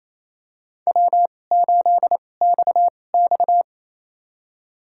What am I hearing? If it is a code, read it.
Morse code, heard as a keyed tone, about 28 wpm, spaced as for 20 wpm, 700 Hz. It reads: W8XX
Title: USA Call Signs - 2x Character Spacing - 28wpm